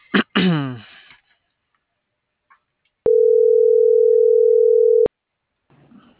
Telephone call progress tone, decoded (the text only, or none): ringback tone